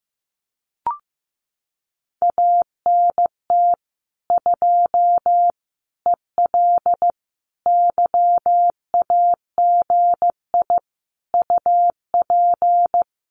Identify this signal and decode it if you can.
Morse code, heard as a keyed tone, about 15 wpm, 700 Hz. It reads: ANT 2 EL YAGI UP
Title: Common QSO Phrases Encore 15wpm